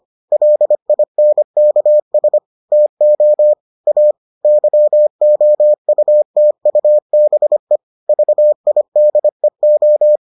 Morse code, heard as a keyed tone, about 25 wpm, 600 Hz. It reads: LINKS TO A YOUTUBE VIDEO